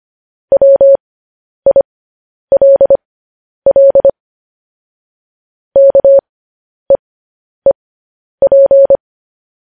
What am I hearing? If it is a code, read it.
Morse code, heard as a keyed tone, about 25 wpm, spaced as for 10 wpm, 550 Hz. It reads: WILL KEEP